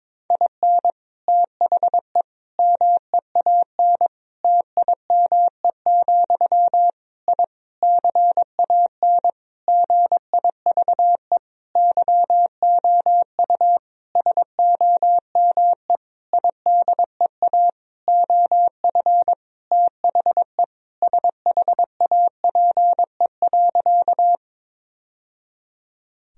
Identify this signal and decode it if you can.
Morse code, heard as a keyed tone, about 21 wpm, 700 Hz. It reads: IN THE MEAN TIME, I CAN GIVE YOU SOME IDEA OF THE SHAPE.
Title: Sentences from Top 1000 Words - Review Part 3 - 22wpm